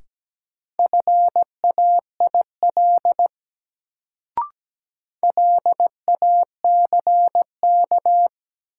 Morse code, heard as a keyed tone, about 17 wpm, 700 Hz. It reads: FAIL LACK
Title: Sentences from Top 1100 Words - New Word Drill - 17wpm